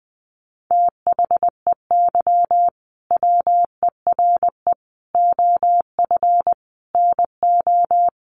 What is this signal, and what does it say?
Morse code, heard as a keyed tone, about 20 wpm, 700 Hz. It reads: THEY WERE OF NO
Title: Sentences from Top 1200 Words 20wpm